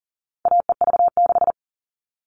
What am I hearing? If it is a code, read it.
Morse code, heard as a keyed tone, about 40 wpm, 700 Hz. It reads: AEV6
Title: All Cut Numbers - 4-Digit 40wpm